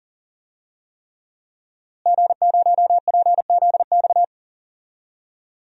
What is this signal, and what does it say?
Morse code, heard as a keyed tone, about 40 wpm, 700 Hz. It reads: G0PZX
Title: DXCC-100 Call Signs - 5 Characters - Repeated 3x - 40wpm